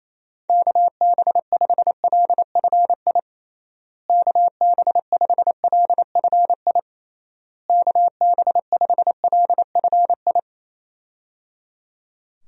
Morse code, heard as a keyed tone, about 28 wpm, 700 Hz. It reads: KB5LFI KB5LFI KB5LFI
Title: USA Call Signs - 6 Characters - Repeated 3x - 28wpm